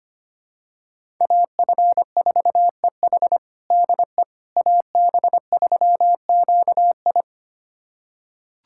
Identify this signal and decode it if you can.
Morse code, heard as a keyed tone, about 25 wpm, 700 Hz. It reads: AF4EH DE AB3QI
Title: USA Callsign DE Callsign - 5 Characters - 25wpm